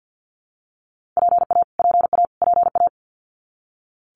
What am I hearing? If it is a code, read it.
Morse code, heard as a keyed tone, about 50 wpm, 700 Hz. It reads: PA PA PA
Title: US States Repeated 3x 50wpm